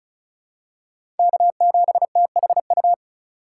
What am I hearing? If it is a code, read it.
Morse code, heard as a keyed tone, about 34 wpm, 700 Hz. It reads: K7THU
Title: USA Call Signs - 5 Characters - 3x Character Spacing - 35wpm